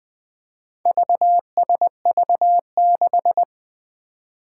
Morse code, heard as a keyed tone, about 20 wpm, 700 Hz. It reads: VSV6
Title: Four Challenging Letter-Number Combinations 20wpm